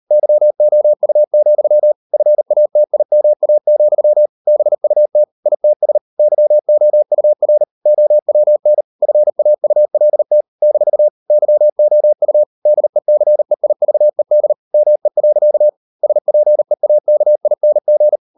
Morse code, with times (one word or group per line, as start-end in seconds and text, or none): 0.1-1.9 YOU,
2.1-4.3 FATIMA,
4.5-5.2 BUT
5.5-6.0 ITS
6.2-7.6 YOUR
7.9-8.8 OWN
9.0-10.4 FAULT
10.6-11.1 -
11.3-12.4 YOU
12.7-14.5 DECEIVED
14.7-15.7 ME.
16.0-18.2 SPEAKING